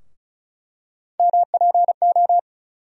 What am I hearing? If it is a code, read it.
Morse code, heard as a keyed tone, about 34 wpm, 700 Hz. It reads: MPO
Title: USA Call Sign Postfixes - 3 Letters 35wpm